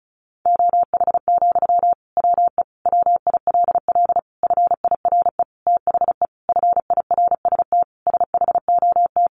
Morse code, heard as a keyed tone, about 34 wpm, 700 Hz. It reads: OH, WE WILL FIRE THE FIRST SHOT
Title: Sentences from Top 1100 Words 35wpm